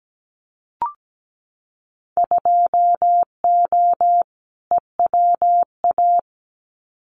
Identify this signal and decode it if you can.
Morse code, heard as a keyed tone, about 17 wpm, 700 Hz. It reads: 2O EWA
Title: Winter Field Day Exchanges Encore 17wpm